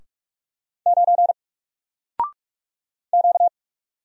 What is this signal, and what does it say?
Morse code, heard as a keyed tone, about 44 wpm, 700 Hz. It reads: 9 Q
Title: Single Letter-Number 45wpm